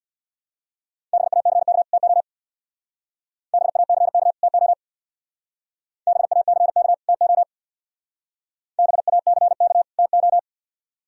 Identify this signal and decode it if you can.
Morse code, heard as a keyed tone, about 64 wpm, 700 Hz. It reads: BACK TO BACK TO BACK TO BACK TO